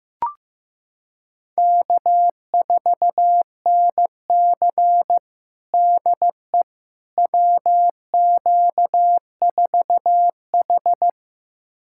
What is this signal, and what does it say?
Morse code, heard as a keyed tone, about 16 wpm, 700 Hz. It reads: K4NC DE WQ4H
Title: USA Callsign DE Callsign - 4 Characters - 15wpm